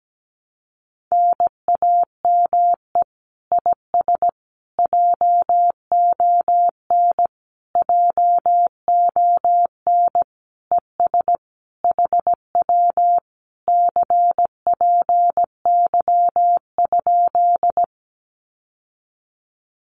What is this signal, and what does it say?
Morse code, heard as a keyed tone, about 17 wpm, 700 Hz. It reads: NAME IS JON JON ES HW CPY?
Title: Common QSO Phrases Encore 17wpm